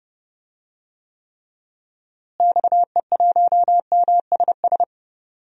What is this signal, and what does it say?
Morse code, heard as a keyed tone, about 31 wpm, 700 Hz. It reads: XE1MSS
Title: DXCC-100 Call Signs - 6 Characters - Repeated 3x - 30wpm